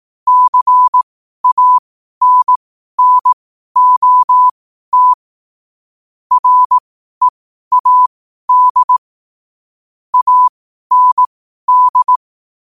Morse code, heard as a keyed tone, about 18 wpm, spaced as for 12 wpm, 1000 Hz. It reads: CANNOT READ AND
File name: 12wpm